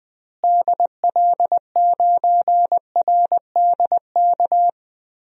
Morse code, heard as a keyed tone, about 20 wpm, 700 Hz. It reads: DL9RDK
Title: DXCC-100 Call Signs - 6 Characters - Repeated 3x - 20wpm